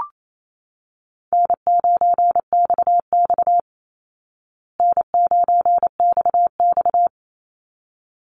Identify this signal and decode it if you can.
Morse code, heard as a keyed tone, about 28 wpm, 700 Hz. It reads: N9XX N9XX
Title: Easy Call Signs Repeated 2x - 28wpm